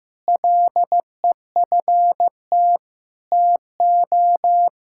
Morse code, heard as a keyed tone, about 15 wpm, 700 Hz. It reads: LEFT TO
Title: Sentences from Top 1000 Words - Review Part 6 - 15wpm